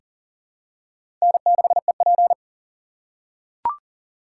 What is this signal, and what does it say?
Morse code, heard as a keyed tone, about 40 wpm, 700 Hz. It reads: N6EP
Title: USA Call Signs - 8x Character Spacing - 40wpm